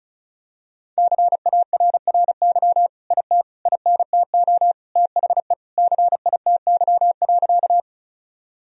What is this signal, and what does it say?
Morse code, heard as a keyed tone, about 36 wpm, 700 Hz. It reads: CARRY IT INTO THE CITY.
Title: Sentences from Top 1000 Words - Review Part 1 - 35wpm